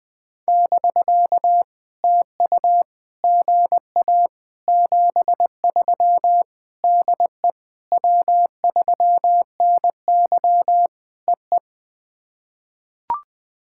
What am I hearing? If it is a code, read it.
Morse code, heard as a keyed tone, about 20 wpm, 700 Hz. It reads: <BK> TU GA 73 DE W3NY EE
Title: POTA QSO Practice - 20wpm